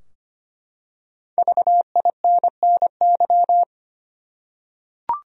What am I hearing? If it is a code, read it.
Morse code, heard as a keyed tone, about 25 wpm, 700 Hz. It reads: VINNY
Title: USA Common Names - Max 5 Letters - 25wpm